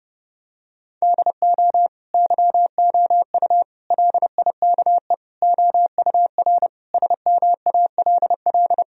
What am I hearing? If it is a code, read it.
Morse code, heard as a keyed tone, about 30 wpm, 700 Hz. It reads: DO YOU LIKE OUR SMALL
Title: Sentences from Top 1000 Words - Review Part 1 - 30wpm